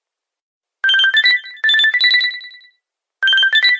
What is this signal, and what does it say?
There is no code.